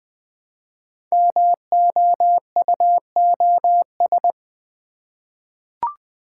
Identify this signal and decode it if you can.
Morse code, heard as a keyed tone, about 20 wpm, 700 Hz. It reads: MOUOS
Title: Code Groups - 5 Letters - 20wpm